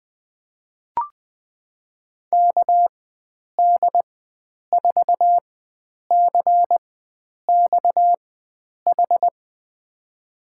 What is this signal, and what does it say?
Morse code, heard as a keyed tone, about 20 wpm, spaced as for 9 wpm, 700 Hz. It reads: KD4CXH
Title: USA Call Signs - 6 Characters - 4x Character Spacing - 20wpm